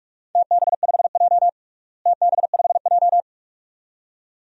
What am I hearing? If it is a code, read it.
Morse code, heard as a keyed tone, about 41 wpm, 700 Hz. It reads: TB5J TB5J
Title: DXCC-100 Call Signs - 4 Characters - Repeated 2x - 45wpm